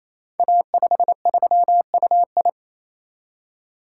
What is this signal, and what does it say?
Morse code, heard as a keyed tone, about 27 wpm, 700 Hz. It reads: A53UI
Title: Code Groups - 5 Letter:Numbers - 28wpm